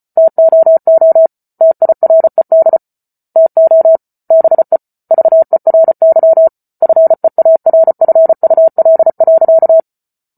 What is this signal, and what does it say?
Morse code, heard as a keyed tone, about 34 wpm, 650 Hz. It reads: TOO TIRED TO BE VERY FEARFUL.